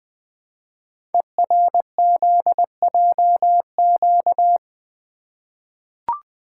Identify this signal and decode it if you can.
Morse code, heard as a keyed tone, about 20 wpm, 700 Hz. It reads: ERZJQ
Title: Code Groups - 5 Letters - 20wpm